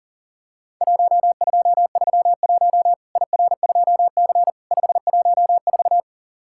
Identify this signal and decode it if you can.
Morse code, heard as a keyed tone, about 37 wpm, 700 Hz. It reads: 1231 IR2C 514
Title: Worked All Europe DX Contest - QTC Exchange - 40wpm